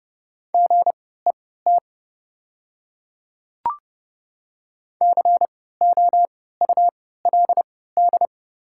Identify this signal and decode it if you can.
Morse code, heard as a keyed tone, about 30 wpm, spaced as for 17 wpm, 700 Hz. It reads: GET COULD
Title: 3x Character Spacing - Top 500 Words - 30wpm